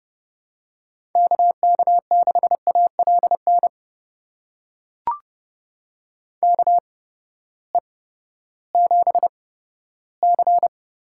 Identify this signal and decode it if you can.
Morse code, heard as a keyed tone, about 32 wpm, spaced as for 15 wpm, 700 Hz. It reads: KK6ALN KE7C
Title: USA Call Signs - 6 Characters - 8x Character Spacing - 30wpm